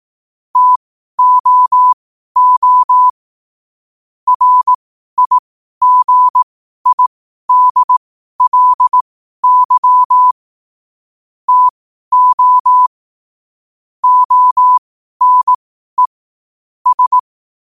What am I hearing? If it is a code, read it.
Morse code, heard as a keyed tone, about 18 wpm, spaced as for 12 wpm, 1000 Hz. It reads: TOO RIGIDLY TO ONE S
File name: qotdfiles / 12wpm